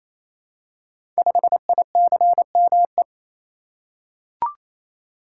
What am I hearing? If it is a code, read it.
Morse code, heard as a keyed tone, about 27 wpm, 700 Hz. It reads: 5ICME
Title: Code Groups - 5 Letter:Numbers - 28wpm